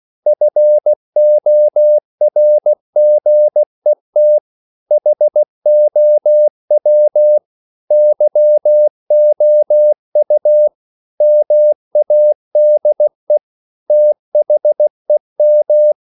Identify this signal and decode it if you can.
Morse code, heard as a keyed tone, about 16 wpm, 600 Hz. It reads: FORGET HOW YOU MADE THEM
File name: CW / Practice / EnglishQuotations0216